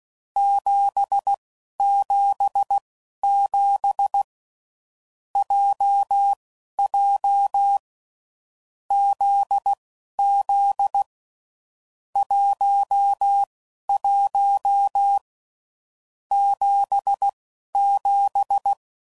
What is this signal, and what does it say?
Morse code, heard as a keyed tone, about 16 wpm, spaced as for 12 wpm, 800 Hz. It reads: 777 JJ ZZ 11 77
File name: Groups / Basic Letter Groups / JZ17